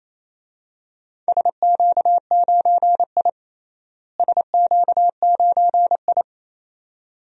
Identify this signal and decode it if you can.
Morse code, heard as a keyed tone, about 28 wpm, 700 Hz. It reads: SQ9I SQ9I
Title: DXCC-100 Call Signs - 4 Characters - Repeated 2x - 28wpm